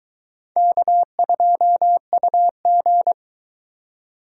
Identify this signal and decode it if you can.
Morse code, heard as a keyed tone, about 20 wpm, 700 Hz. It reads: K2UG